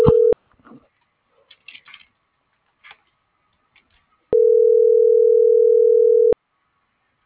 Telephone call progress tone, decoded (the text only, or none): ringback tone